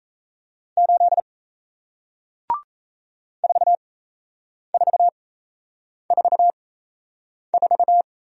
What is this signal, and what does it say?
Morse code, heard as a keyed tone, about 38 wpm, spaced as for 8 wpm, 700 Hz. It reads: 8 4444